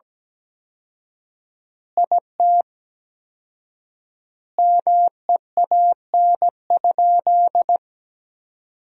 Morse code, heard as a keyed tone, about 17 wpm, 700 Hz. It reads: IT MEAN?
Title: 4x Word-Spacing - Sentences from Top 200 Words - 17wpm